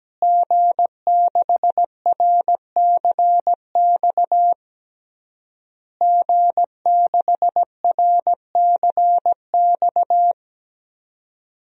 Morse code, heard as a keyed tone, about 17 wpm, 700 Hz. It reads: G6RCX G6RCX
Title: DXCC-100 Call Signs - 5 Characters - Repeated 3x - 17wpm